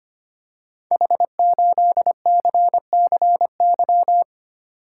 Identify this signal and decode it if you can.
Morse code, heard as a keyed tone, about 25 wpm, 700 Hz. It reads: H8CCY
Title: Code Groups - 5 Letter:Numbers - 25wpm